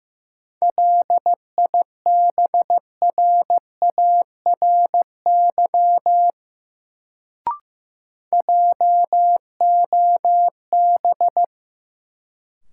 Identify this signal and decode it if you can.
Morse code, heard as a keyed tone, about 15 wpm, 700 Hz. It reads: LIBRARY JOB
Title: Sentences from Top 600 Words - New Word Drill - 15wpm